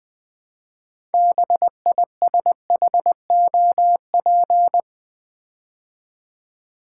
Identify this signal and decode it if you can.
Morse code, heard as a keyed tone, about 20 wpm, 700 Hz. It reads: BISHOP
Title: Build A Word - 2 Letters At A Time - 6-Letter Words - 20wpm